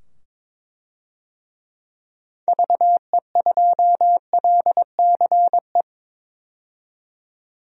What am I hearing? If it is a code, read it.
Morse code, heard as a keyed tone, about 22 wpm, 700 Hz. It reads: VE2LCE
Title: DXCC-100 Call Signs - 6 Characters - 22wpm